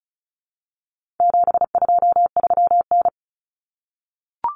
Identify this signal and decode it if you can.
Morse code, heard as a keyed tone, about 34 wpm, 700 Hz. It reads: 723N
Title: Common Cut Numbers - 4-Digit 35wpm